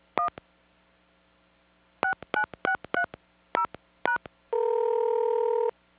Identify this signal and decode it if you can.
DTMF, heard as telephone touch tones, tones of about 100 ms. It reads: 1 69630#